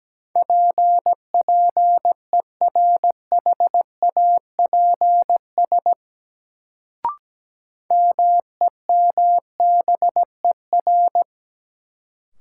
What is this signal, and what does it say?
Morse code, heard as a keyed tone, about 17 wpm, 700 Hz. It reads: PPERHAPS MEMBER
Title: Sentences from Top 1100 Words - New Word Drill - 17wpm